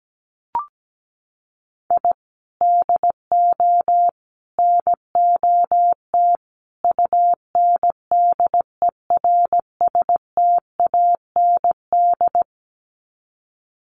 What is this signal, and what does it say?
Morse code, heard as a keyed tone, about 17 wpm, 700 Hz. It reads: I DO NOT UNDERSTAND
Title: Sets of 4 Words 17wpm